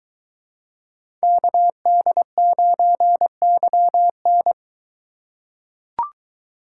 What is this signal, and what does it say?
Morse code, heard as a keyed tone, about 22 wpm, 700 Hz. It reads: KD9YN